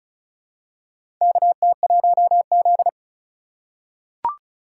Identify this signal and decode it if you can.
Morse code, heard as a keyed tone, about 36 wpm, 700 Hz. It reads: KT1Z
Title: USA Call Signs - 4 Characters - 2x Character Spacing - 35wpm